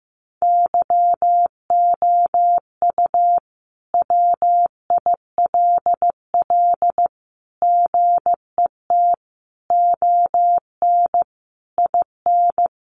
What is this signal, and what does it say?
Morse code, heard as a keyed tone, about 14 wpm, 700 Hz. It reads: YOU WILL GET ON IN
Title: Sentences from Top 200 Words 15wpm